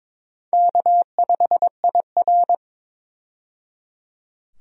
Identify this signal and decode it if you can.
Morse code, heard as a keyed tone, about 22 wpm, 700 Hz. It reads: K5IR
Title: USA Call Signs - 4 Characters - 22wpm